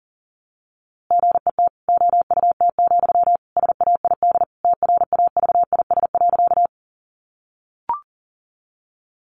Morse code, heard as a keyed tone, about 40 wpm, 700 Hz. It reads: GET OUT, SAID TRAVIS.
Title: A Sound Of Thunder 40wpm